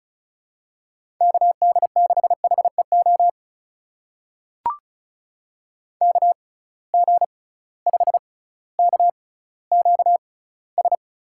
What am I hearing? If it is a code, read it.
Morse code, heard as a keyed tone, about 36 wpm, spaced as for 18 wpm, 700 Hz. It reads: KD6HEO KG5KQS
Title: USA Call Signs - 6 Characters - 6x Character Spacing - 35wpm